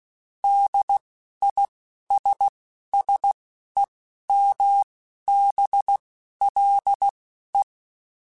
Morse code, heard as a keyed tone, about 16 wpm, spaced as for 11 wpm, 800 Hz. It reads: DISSEMBLE